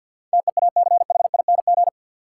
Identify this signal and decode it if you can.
Morse code, heard as a keyed tone, about 53 wpm, 700 Hz. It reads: TEACHING
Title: Sentences from Top 1100 Words - New Word Drill Alt - 50wpm